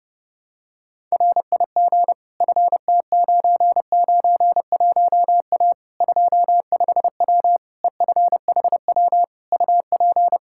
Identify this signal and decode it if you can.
Morse code, heard as a keyed tone, about 30 wpm, 700 Hz. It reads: RIG FT991A 25W EFHW UP
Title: Practice QSO - Exchange 2 - Info - 30wpm